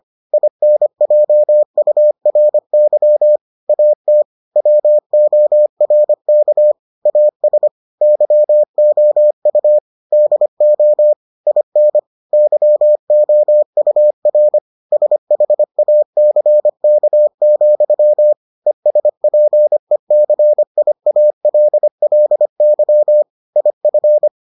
Morse code, heard as a keyed tone, about 25 wpm, 600 Hz. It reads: INJURY AT WORK AS YOU DO IN YOUR SHACK, ESPECIALLY IF